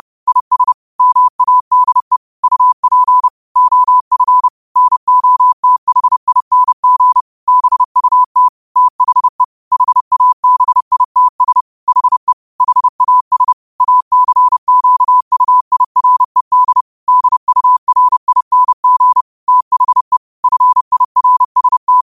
Morse code, heard as a keyed tone, about 30 wpm, 1000 Hz. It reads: IS MADE UP OF NOTHING BUT THE HABITS HE HAS ACQUIRED DURING THE FIRST
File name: qotdfiles / 30wpm